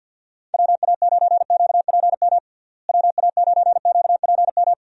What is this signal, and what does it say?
Morse code, heard as a keyed tone, about 49 wpm, 700 Hz. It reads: WA9QPM WA9QPM
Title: USA Call Signs - 6 Characters - Repeated 3x - 50wpm